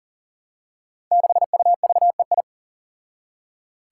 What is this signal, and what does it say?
Morse code, heard as a keyed tone, about 40 wpm, 700 Hz. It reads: 6UVEI
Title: Five Challenging Letter-Number Combinations 40wpm